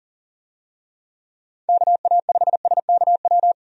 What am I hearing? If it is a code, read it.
Morse code, heard as a keyed tone, about 40 wpm, 700 Hz. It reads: KA5SKW
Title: USA Call Signs - 6 Characters - 40wpm